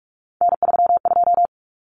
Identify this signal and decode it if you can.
Morse code, heard as a keyed tone, about 44 wpm, 700 Hz. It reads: N32
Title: Common Cut Numbers - 3-Digit 45wpm